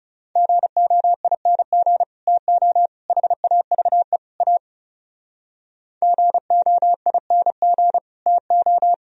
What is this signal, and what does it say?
Morse code, heard as a keyed tone, about 38 wpm, 700 Hz. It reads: GOING TO HAVE A GOING TO